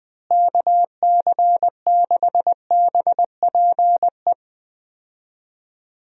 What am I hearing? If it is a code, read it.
Morse code, heard as a keyed tone, about 20 wpm, 700 Hz. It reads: KC6BPE
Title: USA Call Signs - 6 Characters - Repeated 2x - 20wpm